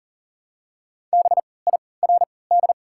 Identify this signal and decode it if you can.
Morse code, heard as a keyed tone, about 40 wpm, spaced as for 23 wpm, 700 Hz. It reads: BIRD
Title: 3x Character Spacing - Top 500 Words - 40wpm